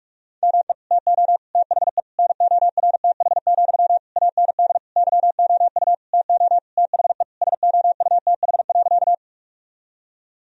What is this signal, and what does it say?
Morse code, heard as a keyed tone, about 46 wpm, 700 Hz. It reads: ME TO THE NORTH, AND YOU TO THE SOUTH.
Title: Sentences from Top 1000 Words - Review Part 2 - 45wpm